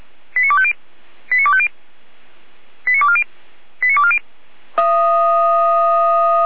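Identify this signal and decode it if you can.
Five-tone selective call (ZVEI), heard as quick runs of five tones, tones of about 70 ms. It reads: 89250 89250 89240 89240
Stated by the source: Five-tone ZVEI selective call 89510, received via SDR.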